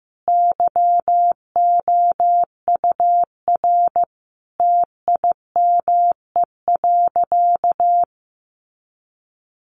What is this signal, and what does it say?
Morse code, heard as a keyed tone, about 15 wpm, 700 Hz. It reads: YOUR TIME.
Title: Sentences from Top 100 Words 15wpm